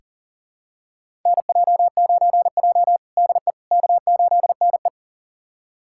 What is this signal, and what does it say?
Morse code, heard as a keyed tone, about 40 wpm, 700 Hz. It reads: NJ9J DE K8NE
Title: USA Callsign DE Callsign - 4 Characters - 40wpm